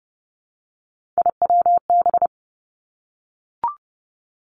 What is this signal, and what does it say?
Morse code, heard as a keyed tone, about 30 wpm, 700 Hz. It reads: IWB
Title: Three Challenging Letter-Number Combinations 30wpm